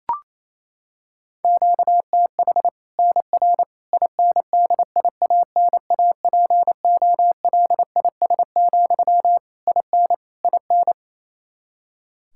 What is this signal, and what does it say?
Morse code, heard as a keyed tone, about 28 wpm, 700 Hz. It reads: QTH NR INDIANAPOLIS, IN IN
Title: Practice QSO - Exchange 1 - Beginning - 28wpm